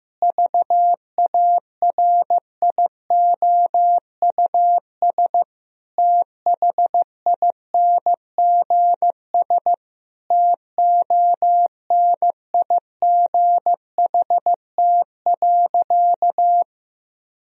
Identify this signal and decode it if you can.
Morse code, heard as a keyed tone, about 15 wpm, 700 Hz. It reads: VARIOUS THINGS TONIGHT.
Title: Sentences from Top 1000 Words - Review Part 4 - 15wpm